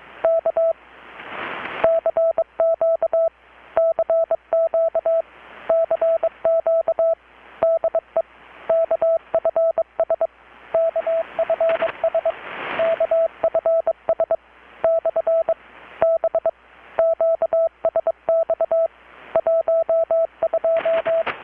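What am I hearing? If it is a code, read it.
Morse code, heard as a keyed tone, about 22 wpm, 660 Hz, with radio noise. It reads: K CQ CQ CQ DE KFS KFS KFS / B QSX 12
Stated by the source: Morse from KPH's Night of Nights XX maritime radiotelegraphy event, received in Chiapas, Mexico about 3,800 km away on a Kenwood R-600 with a grounded 10 m long-wire antenna.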